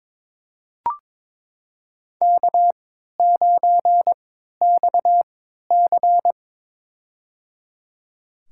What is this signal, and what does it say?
Morse code, heard as a keyed tone, about 22 wpm, spaced as for 13 wpm, 700 Hz. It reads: K9XC
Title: USA Call Signs - 4 Characters - 3x Character Spacing - 22wpm